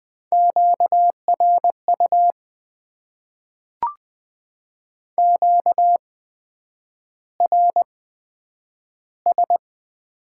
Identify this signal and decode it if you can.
Morse code, heard as a keyed tone, about 20 wpm, spaced as for 9 wpm, 700 Hz. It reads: QRU QRS
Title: Top 90 QSO Elements - 8x Character Spacing - 20wpm